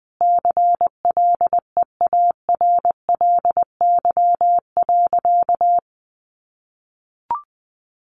Morse code, heard as a keyed tone, about 20 wpm, 700 Hz. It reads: CLEARLY.
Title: Sentences from Top 1300 Words 20wpm